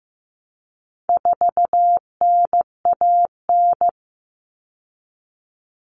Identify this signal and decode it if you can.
Morse code, heard as a keyed tone, about 15 wpm, 700 Hz. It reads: 4NAN
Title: Common Cut Numbers - 4-Digit 15wpm